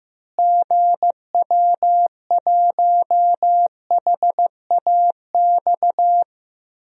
Morse code, heard as a keyed tone, about 16 wpm, 700 Hz. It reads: GW1HAX
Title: DXCC-100 Call Signs - 6 Characters - Repeated 3x - 15wpm